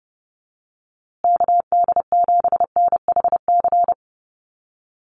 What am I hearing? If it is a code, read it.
Morse code, heard as a keyed tone, about 30 wpm, 700 Hz. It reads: KD7NHC
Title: USA Call Signs Repeated 3x 30wpm